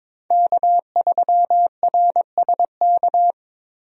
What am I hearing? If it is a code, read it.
Morse code, heard as a keyed tone, about 21 wpm, 700 Hz. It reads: K3RSK
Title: USA Call Signs - 5 Characters - 22wpm